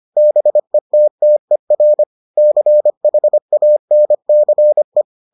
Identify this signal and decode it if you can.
Morse code, heard as a keyed tone, about 25 wpm, 600 Hz. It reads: BETTER CHANCE